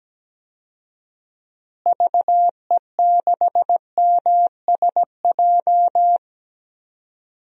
Morse code, heard as a keyed tone, about 17 wpm, 700 Hz. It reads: VE6MSJ
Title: DXCC-100 Call Signs - 6 Characters - Repeated 3x - 17wpm